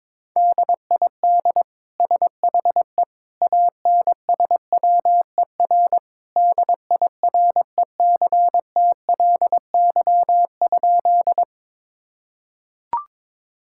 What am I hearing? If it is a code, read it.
Morse code, heard as a keyed tone, about 22 wpm, 700 Hz. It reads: DID SHE ANSWER DIRECTLY?
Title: Sentences from Top 1000 Words - Review Part 5 - 22wpm